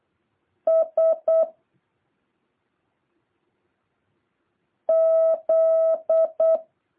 Morse code, heard as a keyed tone, about 8 wpm, 650 Hz. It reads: S Z
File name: jh8cbh / 2023cw